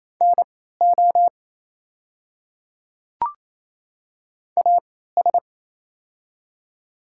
Morse code, heard as a keyed tone, about 28 wpm, spaced as for 16 wpm, 690 Hz. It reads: NO AS
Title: 3x Character Spacing - Top 100 Words - 28wpm